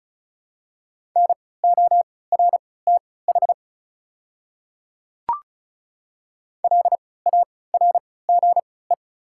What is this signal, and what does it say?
Morse code, heard as a keyed tone, about 34 wpm, spaced as for 20 wpm, 690 Hz. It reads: NORTH LARGE
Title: 3x Character Spacing - Top 300 Words - 35wpm